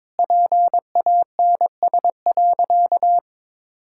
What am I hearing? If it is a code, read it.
Morse code, heard as a keyed tone, about 22 wpm, 700 Hz. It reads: PANS.
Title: Pangram Challenge v2 - 22wpm